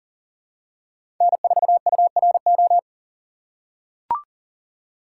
Morse code, heard as a keyed tone, about 38 wpm, 700 Hz. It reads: N4URO
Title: USA Call Signs - 5 Characters - 4x Character Spacing - 40wpm